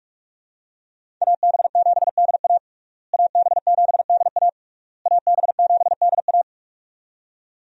Morse code, heard as a keyed tone, about 46 wpm, 700 Hz. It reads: AB7DA AB7DA AB7DA
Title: USA Call Signs - 5 Characters - Repeated 3x - 45wpm